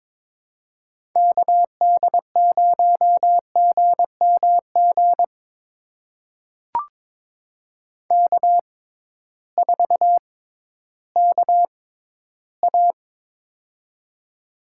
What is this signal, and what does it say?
Morse code, heard as a keyed tone, about 22 wpm, spaced as for 13 wpm, 700 Hz. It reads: KD0GMG K4KA
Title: USA Call Signs - 6x Character Spacing - 22wpm